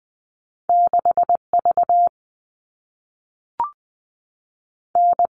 Morse code, heard as a keyed tone, about 20 wpm, 700 Hz. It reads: 6 V N
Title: All Cut Numbers - 2-Digit 20wpm